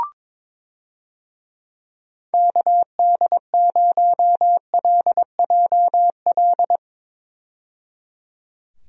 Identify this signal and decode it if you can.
Morse code, heard as a keyed tone, about 22 wpm, 700 Hz. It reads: KD0LJL
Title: USA Call Signs - 6 Characters - 22wpm